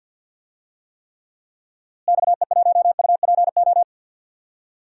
Morse code, heard as a keyed tone, about 52 wpm, 700 Hz. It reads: XE1UPO
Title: DXCC-100 Call Signs - 6 Characters - Repeated 3x - 50wpm